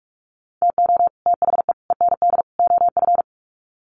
Morse code, heard as a keyed tone, about 44 wpm, 700 Hz. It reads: TO THE END OF
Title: Sets of 4 Words 45wpm